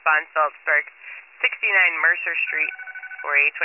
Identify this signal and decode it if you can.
Emergency Alert System (SAME) tones, heard as harsh buzzing bursts, with speech. It is heard from 0.6 to 3.3 s.